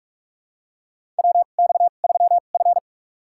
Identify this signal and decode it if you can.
Morse code, heard as a keyed tone, about 44 wpm, spaced as for 32 wpm, 690 Hz. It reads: WX3F
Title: USA Call Signs - 4 Characters - 2x Character Spacing - 45wpm